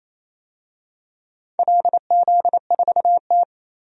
Morse code, heard as a keyed tone, about 28 wpm, 700 Hz. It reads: LZ4T
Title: DXCC-100 Call Signs - 4 Characters - 28wpm